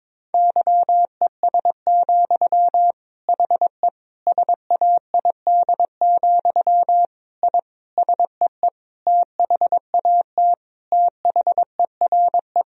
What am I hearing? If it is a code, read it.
Morse code, heard as a keyed tone, about 22 wpm, 700 Hz. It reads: YES, HE SAID, I SEE THAT THERE
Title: Sentences from Top 1000 Words - Review Part 1 - 22wpm